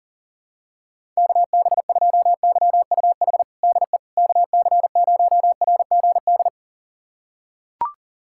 Morse code, heard as a keyed tone, about 40 wpm, 700 Hz. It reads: KB2YUH DE KC0RGD
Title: USA Callsign DE Callsign - 6 Characters - 40wpm